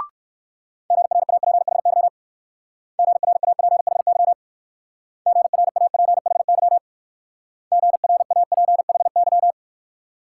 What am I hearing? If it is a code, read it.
Morse code, heard as a keyed tone, about 58 wpm, 700 Hz. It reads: GRAPHY GRAPHY GRAPHY GRAPHY